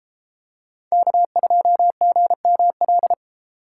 Morse code, heard as a keyed tone, about 32 wpm, 700 Hz. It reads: K2GML